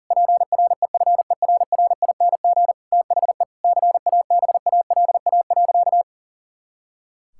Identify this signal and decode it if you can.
Morse code, heard as a keyed tone, about 40 wpm, 700 Hz. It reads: PREFERRING THE CABALA.
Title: Pangram Challenge v2 - 40wpm